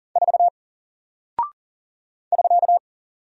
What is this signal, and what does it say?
Morse code, heard as a keyed tone, about 40 wpm, 700 Hz. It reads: 4 <SK>
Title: Single Character (Letter:Number:Punctuation:Prosign) 40wpm